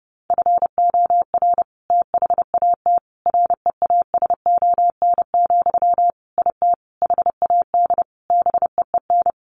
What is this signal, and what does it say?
Morse code, heard as a keyed tone, about 30 wpm, 700 Hz. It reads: FOR THAT REASON, IT HAD BEEN
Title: Sentences from Top 1500 Words 30wpm